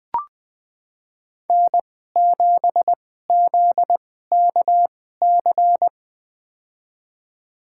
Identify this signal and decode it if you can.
Morse code, heard as a keyed tone, about 20 wpm, spaced as for 14 wpm, 700 Hz. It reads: N7ZKC
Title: USA Call Signs - 5 Characters - 2x Character Spacing - 20wpm